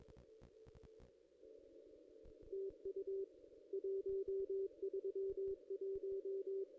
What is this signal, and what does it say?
Morse code, heard as a keyed tone, about 22 wpm, 390 Hz. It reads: TU 131
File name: contests / ss-cw-k5tr-op-n5rz -1700